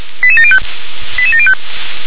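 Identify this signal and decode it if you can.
Five-tone selective call (ZVEI), heard as quick runs of five tones, tones of about 70 ms. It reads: 88705 88705
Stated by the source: ZVEI five-tone selective call received on an SDR.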